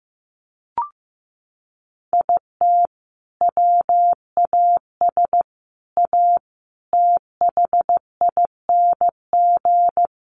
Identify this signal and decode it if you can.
Morse code, heard as a keyed tone, about 15 wpm, 700 Hz. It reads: IT WAS A THING
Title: Sentences from Top 1400 Words 15wpm